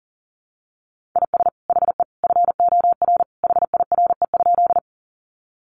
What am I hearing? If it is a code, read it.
Morse code, heard as a keyed tone, about 40 wpm, 700 Hz. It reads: IS HE FOR HIRE?
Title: Sentences from Top 1400 Words 40wpm